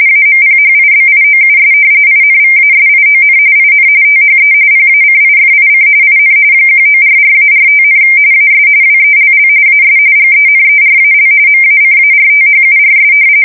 RTTY, about 45.45 baud. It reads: AMPHIBIOUS INVASION IN THE PACIFIC IN 1944-45, AS WELL AS GENERAL MACARTHURS
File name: fsk170 / battleshipnj